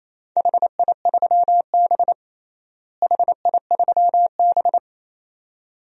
Morse code, heard as a keyed tone, about 28 wpm, 700 Hz. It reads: HI3B HI3B
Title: DXCC-100 Call Signs - 4 Characters - Repeated 2x - 28wpm